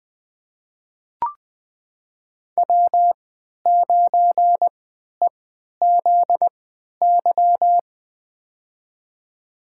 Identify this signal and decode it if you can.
Morse code, heard as a keyed tone, about 20 wpm, spaced as for 11 wpm, 700 Hz. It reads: W9EZY
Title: USA Call Signs - 5 Characters - 3x Character Spacing - 20wpm